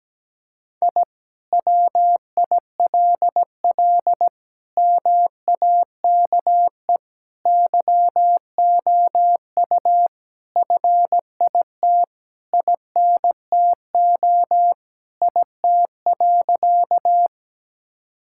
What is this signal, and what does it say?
Morse code, heard as a keyed tone, about 17 wpm, 700 Hz. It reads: I WILL MAKE YOU FIT INTO IT.
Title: Sentences from Top 1000 Words - Review Part 5 - 17wpm